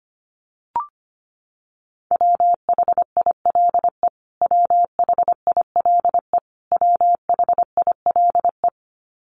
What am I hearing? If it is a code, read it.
Morse code, heard as a keyed tone, about 25 wpm, 700 Hz. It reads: WHILE WHILE WHILE
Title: Top 500 Words Repeated 25wpm